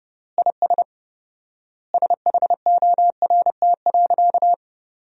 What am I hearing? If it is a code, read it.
Morse code, heard as a keyed tone, about 30 wpm, 700 Hz. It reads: IS SHORT.
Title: 4x Word-Spacing - Sentences from Top 500 Words - 30wpm